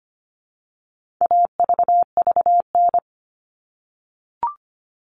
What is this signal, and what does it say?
Morse code, heard as a keyed tone, about 25 wpm, 700 Hz. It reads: AVVN
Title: All Cut Numbers - 4-Digit 25wpm